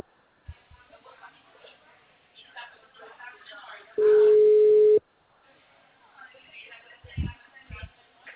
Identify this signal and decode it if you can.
Ringback tone (Europe, 425 Hz), with speech.